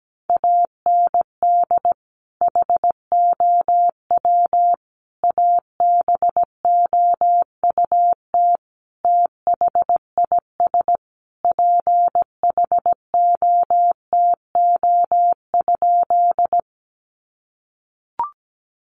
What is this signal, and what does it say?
Morse code, heard as a keyed tone, about 17 wpm, 700 Hz. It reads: AND HOW ABOUT THIS PHOTO?
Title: Sentences from Top 1200 Words 17wpm